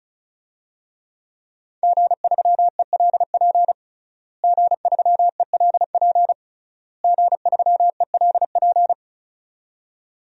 Morse code, heard as a keyed tone, about 36 wpm, 700 Hz. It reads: G3ELP G3ELP G3ELP
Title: DXCC-100 Call Signs - 5 Characters - Repeated 3x - 35wpm